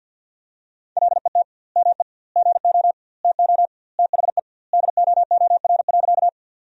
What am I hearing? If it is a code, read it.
Morse code, heard as a keyed tone, about 53 wpm, 700 Hz. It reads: LET ME GO TO THE DOOR.
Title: 2x Word-Spacing - Sentences from Top 300 Words - 50wpm